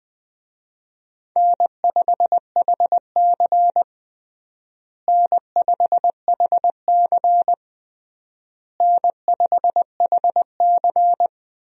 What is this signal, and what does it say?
Morse code, heard as a keyed tone, about 21 wpm, 700 Hz. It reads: N5HC N5HC N5HC
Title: USA Call Signs - 4 Characters - Repeated 3x - 20wpm